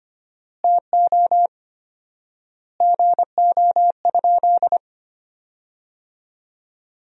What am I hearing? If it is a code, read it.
Morse code, heard as a keyed tone, about 25 wpm, 700 Hz. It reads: TO GO?
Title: 4x Word-Spacing - Sentences from Top 200 Words - 25wpm